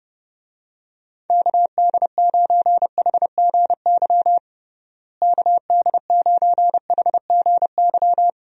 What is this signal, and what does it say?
Morse code, heard as a keyed tone, about 30 wpm, 700 Hz. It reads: KD9HGY KD9HGY
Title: USA Call Signs - 6 Characters - Repeated 2x - 30wpm